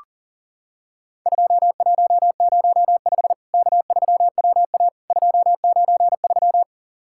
Morse code, heard as a keyed tone, about 40 wpm, 700 Hz. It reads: 2105 K3WA 293
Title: Worked All Europe DX Contest - QTC Exchange - 40wpm